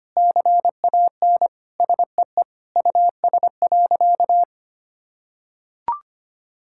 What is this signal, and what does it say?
Morse code, heard as a keyed tone, about 25 wpm, 700 Hz. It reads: CAN SEE US.
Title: Sentences from Top 1600 Words 25wpm